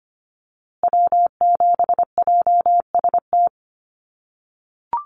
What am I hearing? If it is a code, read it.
Morse code, heard as a keyed tone, about 25 wpm, 700 Hz. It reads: W7JST
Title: USA Call Signs Repeated 2x 25wpm